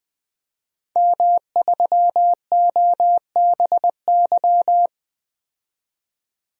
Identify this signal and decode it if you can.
Morse code, heard as a keyed tone, about 20 wpm, 700 Hz. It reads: M3OBY
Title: DXCC-100 Call Signs - 5 Characters - 20wpm